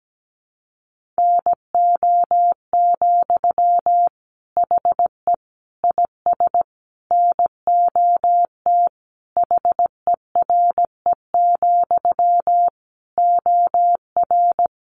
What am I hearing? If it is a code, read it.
Morse code, heard as a keyed tone, about 17 wpm, 700 Hz. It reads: NO, HE IS NOT HERE, OR
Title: Sentences from Top 1400 Words 17wpm